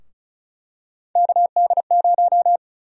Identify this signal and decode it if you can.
Morse code, heard as a keyed tone, about 34 wpm, 700 Hz. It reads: KD0
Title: USA Call Sign Prefixes - 35wpm